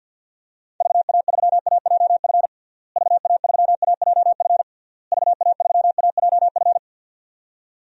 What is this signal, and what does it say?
Morse code, heard as a keyed tone, about 50 wpm, 700 Hz. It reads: VA3AJF VA3AJF VA3AJF
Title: DXCC-100 Call Signs - 6 Characters - Repeated 3x - 50wpm